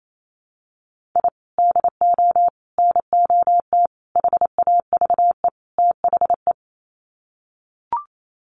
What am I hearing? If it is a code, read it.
Morse code, heard as a keyed tone, about 28 wpm, 700 Hz. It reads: I DO NOT HAVE THE
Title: Sets of 5 Words 28wpm